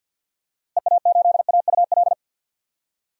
Morse code, heard as a keyed tone, about 50 wpm, 700 Hz. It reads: EA8AUL
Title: DXCC-100 Call Signs - 6 Characters - Repeated 3x - 50wpm